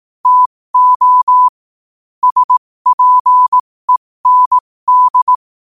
Morse code, heard as a keyed tone, about 18 wpm, spaced as for 15 wpm, 1000 Hz. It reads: TO SPEND
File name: qotdfiles / 15wpm